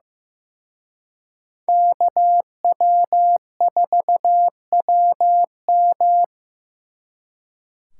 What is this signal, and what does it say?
Morse code, heard as a keyed tone, about 18 wpm, 700 Hz. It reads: KW4WM
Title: Speed Racing - USA Call Signs 15wpm